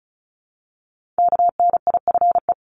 Morse code, heard as a keyed tone, about 36 wpm, 700 Hz. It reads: KNIFE
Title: Multi-letter Phonemes Repeated 3x 35wpm